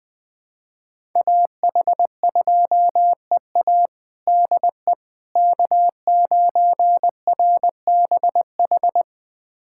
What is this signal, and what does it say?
Morse code, heard as a keyed tone, about 20 wpm, 700 Hz. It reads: AH2EA DE K9RBH
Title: USA Callsign DE Callsign - 5 Characters - 20wpm